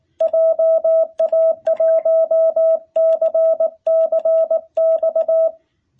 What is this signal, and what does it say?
Morse code, heard as a keyed tone, about 19 wpm, 660 Hz. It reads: JA1CCX
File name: jh8cbh / 2021cw